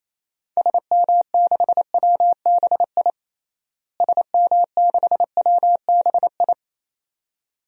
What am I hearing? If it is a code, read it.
Morse code, heard as a keyed tone, about 28 wpm, 700 Hz. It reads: SM6WBI SM6WBI
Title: DXCC-100 Call Signs - 6 Characters - Repeated 2x - 28wpm